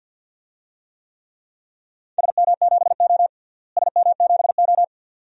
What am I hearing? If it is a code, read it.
Morse code, heard as a keyed tone, about 52 wpm, 700 Hz. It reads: SM7O SM7O
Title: DXCC-100 Call Signs - 4 Characters - Repeated 2x - 50wpm